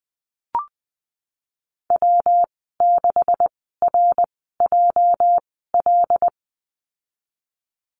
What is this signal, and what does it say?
Morse code, heard as a keyed tone, about 20 wpm, spaced as for 15 wpm, 700 Hz. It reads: W6RJL
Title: 238 Lesson 38g - Letter q - Call Signs 20wpm